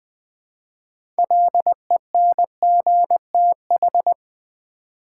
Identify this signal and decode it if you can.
Morse code, heard as a keyed tone, about 21 wpm, 700 Hz. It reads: LENGTH
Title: Six Letter Words - 20wpm